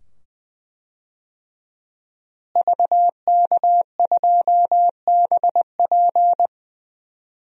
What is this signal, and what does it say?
Morse code, heard as a keyed tone, about 20 wpm, 700 Hz. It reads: VK2BP
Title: DXCC-100 Call Signs - 5 Characters - Repeated 2x - 20wpm